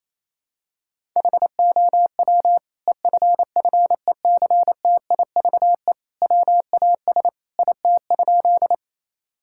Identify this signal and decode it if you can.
Morse code, heard as a keyed tone, about 28 wpm, 700 Hz. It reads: HOW EFFECTIVE WAS IT?
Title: Sentences from Top 1000 Words - Review Part 6 - 28wpm